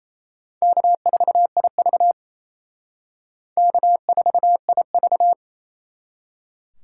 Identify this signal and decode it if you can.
Morse code, heard as a keyed tone, about 35 wpm, 700 Hz. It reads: K4IV K4IV